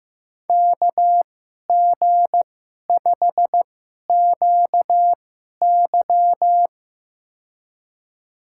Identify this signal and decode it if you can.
Morse code, heard as a keyed tone, about 15 wpm, spaced as for 11 wpm, 700 Hz. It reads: KG5QY
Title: USA Call Signs - 5 Characters - 2x Character Spacing - 15wpm